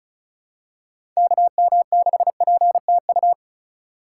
Morse code, heard as a keyed tone, about 36 wpm, 700 Hz. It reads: KM6PTU
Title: USA Call Signs - 6 Characters - 3x Character Spacing - 35wpm